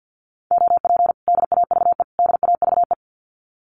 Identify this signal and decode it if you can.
Morse code, heard as a keyed tone, about 50 wpm, 700 Hz. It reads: OP DAVE DAVE
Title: Common QSO Phrases Encore 50wpm